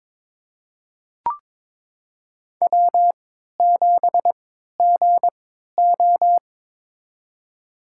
Morse code, heard as a keyed tone, about 22 wpm, spaced as for 13 wpm, 700 Hz. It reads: W7GO
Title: USA Call Signs - 4 Characters - 3x Character Spacing - 22wpm